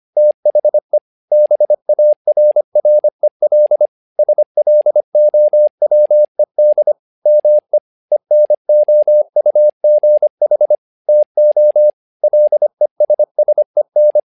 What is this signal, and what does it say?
Morse code, heard as a keyed tone, about 25 wpm, 600 Hz. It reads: THE BARREL SLOWED ME ENOUGH TO LESSEN
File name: learning center / tower trouble / tower trouble 25wpm